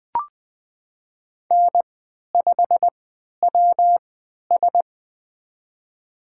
Morse code, heard as a keyed tone, about 20 wpm, spaced as for 11 wpm, 700 Hz. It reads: N5WS
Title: USA Call Signs - 4 Characters - 3x Character Spacing - 20wpm